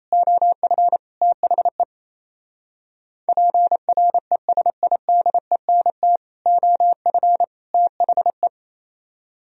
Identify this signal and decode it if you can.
Morse code, heard as a keyed tone, about 31 wpm, 700 Hz. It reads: OF THE PRESIDENT OF THE
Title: Speed-Racing - Sets of 3 Words 28wpm